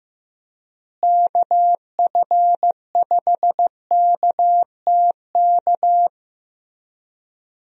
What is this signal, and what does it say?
Morse code, heard as a keyed tone, about 16 wpm, 700 Hz. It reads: KF5KTK
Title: USA Call Signs - 6 Characters - Repeated 3x - 15wpm